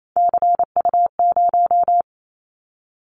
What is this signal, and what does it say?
Morse code, heard as a keyed tone, about 28 wpm, 700 Hz. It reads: CU0
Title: Callsign Prefixes for Top 50 DXCC - 28wpm